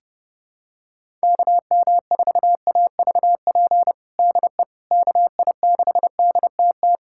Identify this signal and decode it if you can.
Morse code, heard as a keyed tone, about 30 wpm, 700 Hz. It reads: KM4AVP DE KI6DTT
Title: USA Callsign DE Callsign - 6 Characters - 30wpm